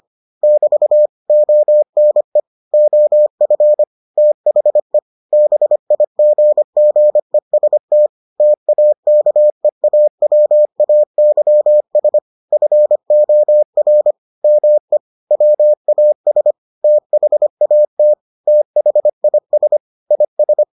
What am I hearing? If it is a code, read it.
Morse code, heard as a keyed tone, about 25 wpm, 600 Hz. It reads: <BT> ONE OF THE BIGGEST TAKEAWAYS FOR ME WAS THAT THIS IS